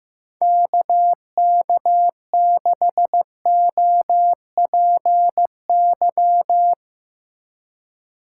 Morse code, heard as a keyed tone, about 17 wpm, 700 Hz. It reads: KK6OPY
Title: Speed Racing - USA 2x3 Call Signs 15wpm